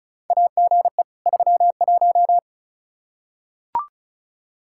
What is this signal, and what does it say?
Morse code, heard as a keyed tone, about 36 wpm, 700 Hz. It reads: AGE 31
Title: K1USN SST QSO Practice 35wpm